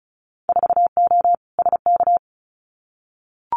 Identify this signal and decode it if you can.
Morse code, heard as a keyed tone, about 34 wpm, 700 Hz. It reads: 4O SK
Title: Winter Field Day Exchanges Encore 35wpm